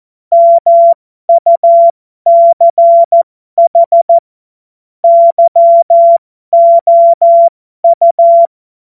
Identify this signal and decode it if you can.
Morse code, heard as a keyed tone, about 14 wpm, spaced as for 12 wpm, 680 Hz. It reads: MUCH YOU